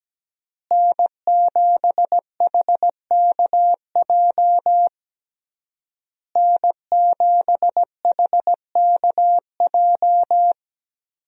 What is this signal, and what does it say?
Morse code, heard as a keyed tone, about 17 wpm, 700 Hz. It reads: N7HKJ N7HKJ
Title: USA Call Signs - 5 Characters - Repeated 3x - 17wpm